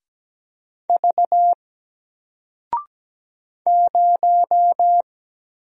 Morse code, heard as a keyed tone, about 17 wpm, 700 Hz. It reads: V 0
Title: Single Letter-Number Alt 17wpm